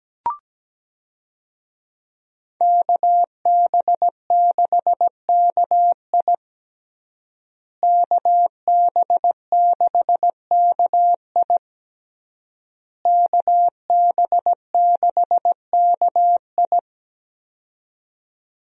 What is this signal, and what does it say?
Morse code, heard as a keyed tone, about 17 wpm, 700 Hz. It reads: KB6KI KB6KI KB6KI
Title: USA Call Signs - 5 Characters - Repeated 3x - 17wpm